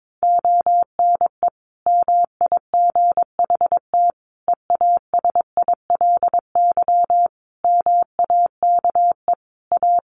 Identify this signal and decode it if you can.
Morse code, heard as a keyed tone, about 22 wpm, 700 Hz. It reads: ONE MIGHT EASILY MAKE A
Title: Sentences from Top 1400 Words 22wpm